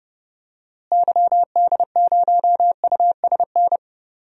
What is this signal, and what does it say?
Morse code, heard as a keyed tone, about 31 wpm, 700 Hz. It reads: YD0USN
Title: DXCC-100 Call Signs - 6 Characters - Repeated 3x - 30wpm